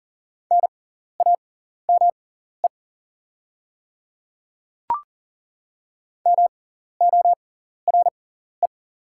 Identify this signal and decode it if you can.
Morse code, heard as a keyed tone, about 37 wpm, spaced as for 14 wpm, 700 Hz. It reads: NAME MORE
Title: 6x Character Spacing - Top 200 Words - 40wpm